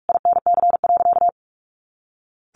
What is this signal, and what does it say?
Morse code, heard as a keyed tone, about 44 wpm, 700 Hz. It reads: INC.
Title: A Sound Of Thunder 45wpm